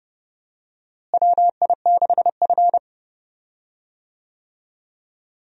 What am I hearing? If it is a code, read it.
Morse code, heard as a keyed tone, about 29 wpm, 700 Hz. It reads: WI6F
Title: USA Call Signs - 4 Characters - Repeated 2x - 30wpm